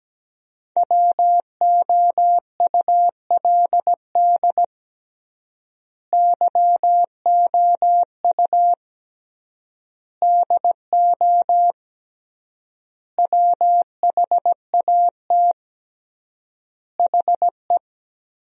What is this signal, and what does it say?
Morse code, heard as a keyed tone, about 17 wpm, 700 Hz. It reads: WOULD YOU DO WHAT HE
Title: 3x Word-Spacing - Sentences from Top 100 Words - 17wpm